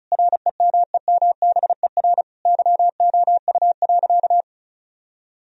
Morse code, heard as a keyed tone, about 34 wpm, 700 Hz. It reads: REMEMBER YOU.
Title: Sentences from Top 1000 Words - Review Part 2 - 35wpm